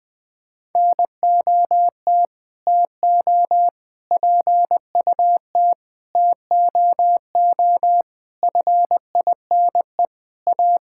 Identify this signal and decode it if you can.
Morse code, heard as a keyed tone, about 20 wpm, 700 Hz. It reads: NOT TO PUT TOO FINE A
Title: Sentences with 9-Word Phrases - 20wpm